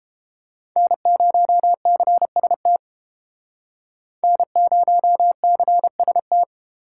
Morse code, heard as a keyed tone, about 29 wpm, 700 Hz. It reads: N0CST N0CST